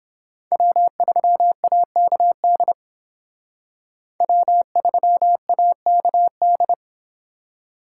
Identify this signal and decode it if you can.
Morse code, heard as a keyed tone, about 29 wpm, 700 Hz. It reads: W3AKD W3AKD